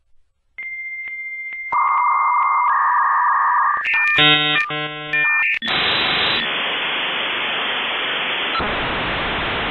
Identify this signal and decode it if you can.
Dial-up modem handshake.
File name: Suono del modem 56k